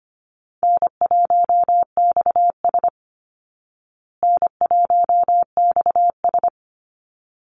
Morse code, heard as a keyed tone, about 25 wpm, 700 Hz. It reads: N1XS N1XS
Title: Easy Call Signs Repeated 2x - 25wpm